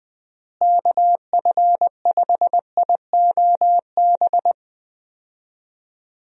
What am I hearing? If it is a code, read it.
Morse code, heard as a keyed tone, about 21 wpm, 700 Hz. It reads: KF5IOB
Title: USA Call Signs - 6 Characters - Repeated 2x - 20wpm